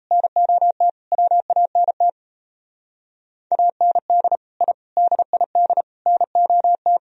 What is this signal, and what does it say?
Morse code, heard as a keyed tone, about 35 wpm, 700 Hz. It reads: NOT WANT AND I DID NOT